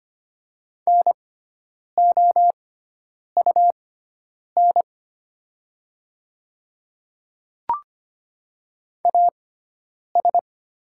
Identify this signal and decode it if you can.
Morse code, heard as a keyed tone, about 25 wpm, spaced as for 9 wpm, 700 Hz. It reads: NOUN AS
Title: 6x Character Spacing - Top 500 Words - 25wpm